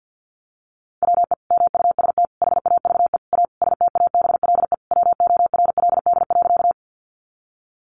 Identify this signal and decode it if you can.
Morse code, heard as a keyed tone, about 50 wpm, 700 Hz. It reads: WE MUST HAVE A STABLE WORLD.
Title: Sentences from Top 1500 Words 50wpm